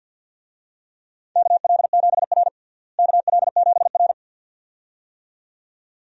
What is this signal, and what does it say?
Morse code, heard as a keyed tone, about 52 wpm, 700 Hz. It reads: KL7R KL7R
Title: DXCC-100 Call Signs - 4 Characters - Repeated 2x - 50wpm